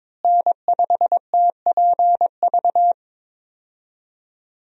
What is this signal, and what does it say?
Morse code, heard as a keyed tone, about 22 wpm, 700 Hz. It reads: N5TPV
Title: USA Call Signs - 5 Characters - Repeated 2x - 22wpm